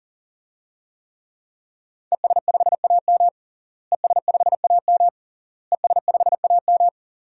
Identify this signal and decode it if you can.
Morse code, heard as a keyed tone, about 40 wpm, 700 Hz. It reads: ES5AM ES5AM ES5AM
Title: DXCC-100 Call Signs - 5 Characters - Repeated 3x - 40wpm